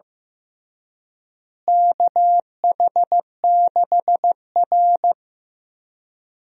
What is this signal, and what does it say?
Morse code, heard as a keyed tone, about 15 wpm, 700 Hz. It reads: KH6R
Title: DXCC-100 Call Signs - 4 Characters - Repeated 2x - 15wpm